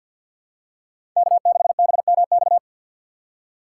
Morse code, heard as a keyed tone, about 52 wpm, 700 Hz. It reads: K6BMX
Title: USA Call Signs - 5 Characters - 3x Character Spacing - 50wpm